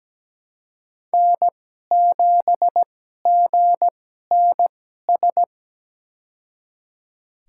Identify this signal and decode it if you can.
Morse code, heard as a keyed tone, about 17 wpm, spaced as for 12 wpm, 700 Hz. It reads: N7GNS
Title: USA Call Signs - 5 Characters - 2x Character Spacing - 17wpm